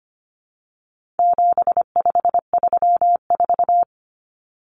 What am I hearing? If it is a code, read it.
Morse code, heard as a keyed tone, about 25 wpm, 700 Hz. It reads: 7534
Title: Common Cut Numbers - 4-Digit 25wpm